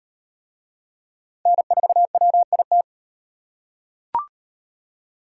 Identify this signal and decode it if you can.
Morse code, heard as a keyed tone, about 37 wpm, 700 Hz. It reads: N4WIT